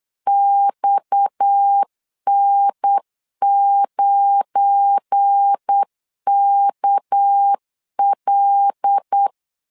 Morse code, heard as a keyed tone, about 8 wpm, 800 Hz. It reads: XN9KL
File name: LEZIONI / 7 GF9Z / 7 GF9 8wpm 3545